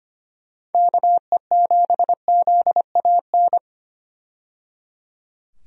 Morse code, heard as a keyed tone, about 25 wpm, 700 Hz. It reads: KE7ZAN
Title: USA Call Signs - 6 Characters - 25wpm